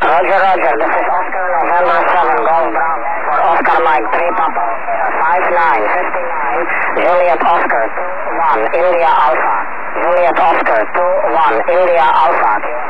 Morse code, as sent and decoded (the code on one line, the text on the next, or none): none